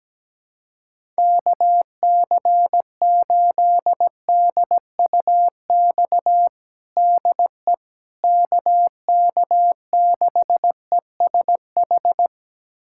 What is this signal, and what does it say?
Morse code, heard as a keyed tone, about 17 wpm, 700 Hz. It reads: KC8DUX DE KK6ESH
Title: USA Callsign DE Callsign 17wpm